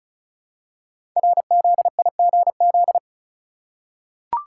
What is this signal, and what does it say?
Morse code, heard as a keyed tone, about 36 wpm, 700 Hz. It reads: RZIGZ
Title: Code Groups - 5 Letters - 35wpm